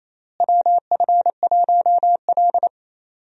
Morse code, heard as a keyed tone, about 28 wpm, 700 Hz. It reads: WF1L
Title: USA Call Signs - 4 Characters - Repeated 3x - 28wpm